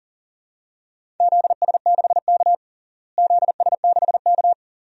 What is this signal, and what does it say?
Morse code, heard as a keyed tone, about 40 wpm, 700 Hz. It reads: ZS6K ZS6K
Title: DXCC-100 Call Signs - 4 Characters - Repeated 2x - 40wpm